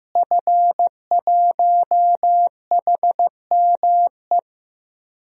Morse code, heard as a keyed tone, about 15 wpm, 700 Hz. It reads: F1HME
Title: DXCC-100 Call Signs - 5 Characters - Repeated 3x - 15wpm